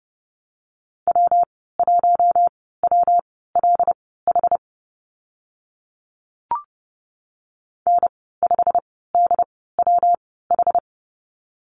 Morse code, heard as a keyed tone, about 32 wpm, spaced as for 17 wpm, 700 Hz. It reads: W1WLH N5DWH
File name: Lesson 19g - Letter w - Call Signs 30wpm